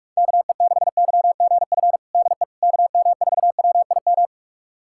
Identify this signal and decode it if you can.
Morse code, heard as a keyed tone, about 46 wpm, 700 Hz. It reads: KE6YGF DE KM4WIM
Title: USA Callsign DE Callsign - 6 Characters - 45wpm